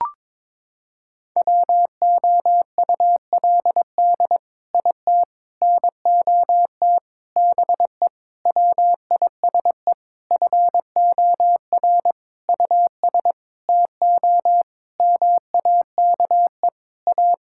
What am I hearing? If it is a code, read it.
Morse code, heard as a keyed tone, about 22 wpm, 700 Hz. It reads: WOULD IT NOT BE WISE FOR US TO MAKE A
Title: Sentences from Top 1600 Words 22wpm